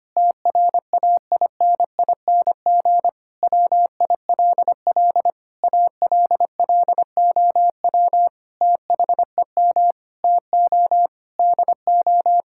Morse code, heard as a keyed tone, about 25 wpm, 700 Hz. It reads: TRAINING WILL ALLOW THEM TO DO
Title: Sentences from Top 1000 Words - Review Part 5 - 25wpm